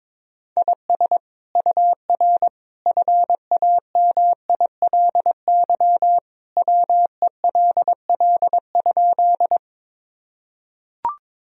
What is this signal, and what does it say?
Morse code, heard as a keyed tone, about 22 wpm, 700 Hz. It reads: IS UR FAMILY WELL?
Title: Sentences from Top 400 Words and Abbreviations 22wpm